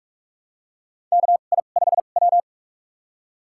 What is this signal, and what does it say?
Morse code, heard as a keyed tone, about 42 wpm, spaced as for 31 wpm, 700 Hz. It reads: KI5W
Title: USA Call Signs - 4 Characters - 2x Character Spacing - 45wpm